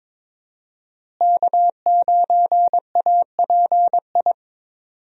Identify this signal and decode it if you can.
Morse code, heard as a keyed tone, about 22 wpm, 700 Hz. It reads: K9API
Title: USA Call Signs - 5 Characters - Repeated 3x - 22wpm